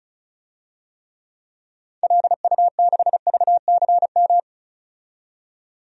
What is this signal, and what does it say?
Morse code, heard as a keyed tone, about 36 wpm, 700 Hz. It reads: LU6VCM
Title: DXCC-100 Call Signs - 6 Characters - 35wpm